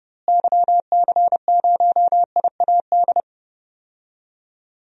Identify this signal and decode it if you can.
Morse code, heard as a keyed tone, about 30 wpm, 700 Hz. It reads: YC0IAD
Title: DXCC-100 Call Signs - 6 Characters - Repeated 2x - 30wpm